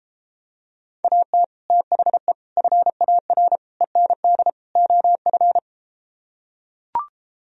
Morse code, heard as a keyed tone, about 33 wpm, 700 Hz. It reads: AT THE FAR END OF